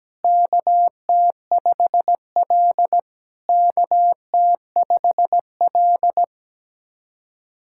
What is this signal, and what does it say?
Morse code, heard as a keyed tone, about 18 wpm, 700 Hz. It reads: KT5L KT5L
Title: SOTA QSO Practice 17wpm